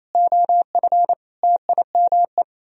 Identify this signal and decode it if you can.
Morse code, heard as a keyed tone, about 34 wpm, 700 Hz. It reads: OF TIME
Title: Speed-Racing - Sets of 4 Words 28wpm